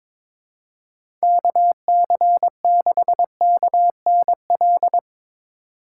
Morse code, heard as a keyed tone, about 22 wpm, 700 Hz. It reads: KC6KNL
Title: USA Call Signs - 6 Characters - 4x Character Spacing - 22wpm